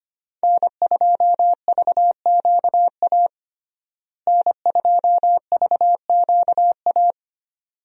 Morse code, heard as a keyed tone, about 25 wpm, 700 Hz. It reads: N2VQA N2VQA
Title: USA Call Signs - 5 Characters - Repeated 2x - 25wpm